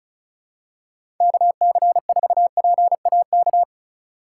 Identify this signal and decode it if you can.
Morse code, heard as a keyed tone, about 36 wpm, 700 Hz. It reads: KC4PAK
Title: USA Call Signs - 6 Characters - 4x Character Spacing - 35wpm